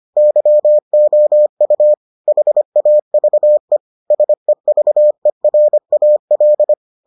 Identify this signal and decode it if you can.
Morse code, heard as a keyed tone, about 25 wpm, 600 Hz. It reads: YOU HAVE SEVERAL